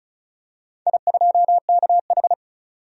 Morse code, heard as a keyed tone, about 36 wpm, 700 Hz. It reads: I2KH
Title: DXCC-100 Call Signs - 4 Characters - 35wpm